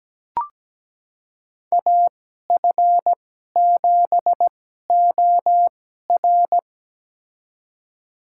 Morse code, heard as a keyed tone, about 17 wpm, spaced as for 12 wpm, 700 Hz. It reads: AF7OR
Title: USA Call Signs - 2x Character Spacing - 17wpm